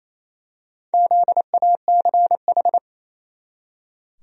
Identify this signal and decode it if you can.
Morse code, heard as a keyed tone, about 27 wpm, 700 Hz. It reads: ZACH
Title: USA Common Names - Max 4 Letters - 28wpm